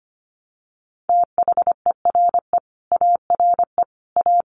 Morse code, heard as a keyed tone, about 25 wpm, 700 Hz. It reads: THERE ARE A
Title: Sets of 5 Words 25wpm